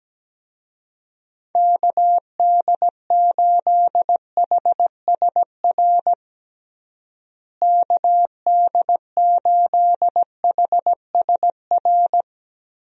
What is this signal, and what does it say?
Morse code, heard as a keyed tone, about 17 wpm, 700 Hz. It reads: KD8HSR KD8HSR
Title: USA Call Signs - 6 Characters - Repeated 2x - 17wpm